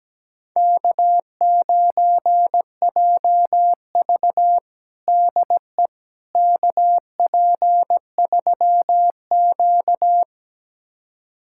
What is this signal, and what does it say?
Morse code, heard as a keyed tone, about 17 wpm, 700 Hz. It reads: K9JV DE KP3Q
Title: USA Callsign DE Callsign - 4 Characters - 17wpm